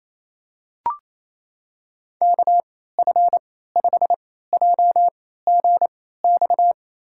Morse code, heard as a keyed tone, about 28 wpm, spaced as for 16 wpm, 700 Hz. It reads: KF5JGX
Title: USA Call Signs - 6 Characters - 3x Character Spacing - 28wpm